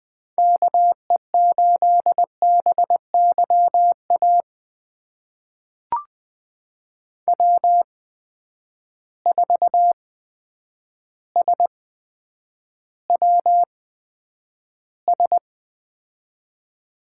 Morse code, heard as a keyed tone, about 20 wpm, spaced as for 9 wpm, 700 Hz. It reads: KE8BYA W4SWS
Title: USA Call Signs - 8x Character Spacing - 20wpm